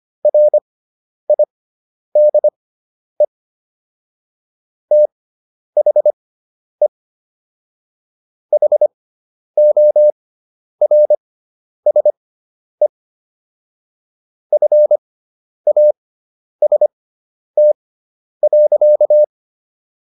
Morse code, heard as a keyed tone, about 25 wpm, spaced as for 10 wpm, 600 Hz. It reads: RIDE THE HORSE FAST.